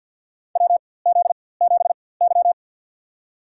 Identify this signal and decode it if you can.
Morse code, heard as a keyed tone, about 53 wpm, spaced as for 24 wpm, 700 Hz. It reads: WZ7Y
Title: USA Call Signs - 4x Character Spacing - 50wpm